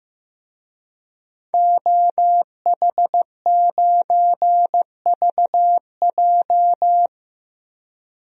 Morse code, heard as a keyed tone, about 15 wpm, 700 Hz. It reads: OH9VJ
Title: DXCC-100 Call Signs - 5 Characters - 15wpm